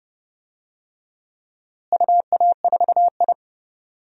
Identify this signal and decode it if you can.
Morse code, heard as a keyed tone, about 30 wpm, 700 Hz. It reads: UA4I
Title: DXCC-100 Call Signs - 4 Characters - Repeated 3x - 30wpm